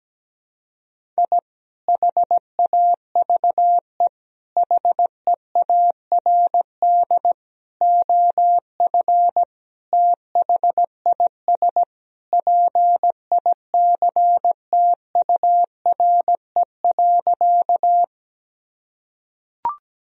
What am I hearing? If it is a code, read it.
Morse code, heard as a keyed tone, about 17 wpm, 700 Hz. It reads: I HAVE HEARD OF THIS PICTURE.
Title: Sentences from Top 1000 Words - Review Part 1 - 17wpm